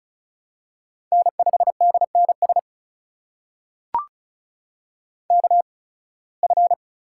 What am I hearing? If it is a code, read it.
Morse code, heard as a keyed tone, about 36 wpm, spaced as for 23 wpm, 700 Hz. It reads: N5DNS KF
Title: USA Call Signs - 8x Character Spacing - 35wpm